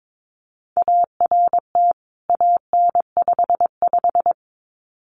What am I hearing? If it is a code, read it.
Morse code, heard as a keyed tone, about 22 wpm, 700 Hz. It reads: ART AN55
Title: CWT Exchange 22wpm